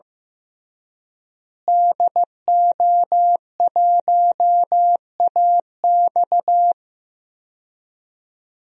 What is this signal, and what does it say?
Morse code, heard as a keyed tone, about 15 wpm, 700 Hz. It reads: DO1AX
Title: DXCC-100 Call Signs - 5 Characters - Repeated 3x - 15wpm